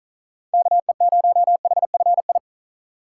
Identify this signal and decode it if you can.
Morse code, heard as a keyed tone, about 43 wpm, 700 Hz. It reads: KE0HFI
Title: Speed Racing - USA Call Signs 35wpm